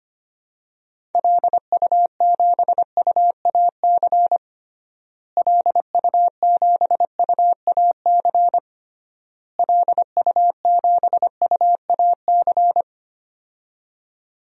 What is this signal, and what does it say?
Morse code, heard as a keyed tone, about 25 wpm, 700 Hz. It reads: LU7UAC LU7UAC LU7UAC
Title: DXCC-100 Call Signs - 6 Characters - Repeated 3x - 25wpm